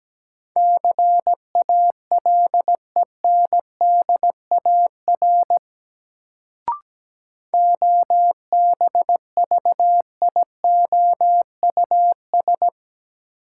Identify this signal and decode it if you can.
Morse code, heard as a keyed tone, about 17 wpm, 700 Hz. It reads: CALENDAR OBVIOUS
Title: Sentences from Top 1600 Words - New Word Drill Alt - 17wpm